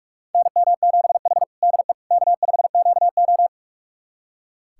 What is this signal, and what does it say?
Morse code, heard as a keyed tone, about 44 wpm, 690 Hz. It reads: NM7H DE K5QO
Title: USA Callsign DE Callsign - 4 Characters - 45wpm